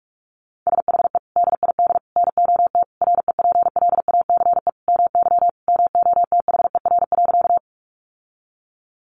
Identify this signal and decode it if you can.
Morse code, heard as a keyed tone, about 44 wpm, 700 Hz. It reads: SHE DID NOT REPLACE MY MOTHER.
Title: Sentences from Top 1400 Words 45wpm